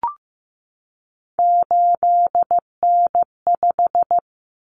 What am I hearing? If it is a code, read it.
Morse code, heard as a keyed tone, about 15 wpm, 700 Hz. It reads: 8N5
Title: Common Cut Numbers - 3-Digit 15wpm